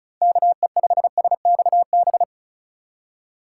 Morse code, heard as a keyed tone, about 36 wpm, 700 Hz. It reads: KE5SXB
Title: USA Call Signs - 6 Characters - 3x Character Spacing - 35wpm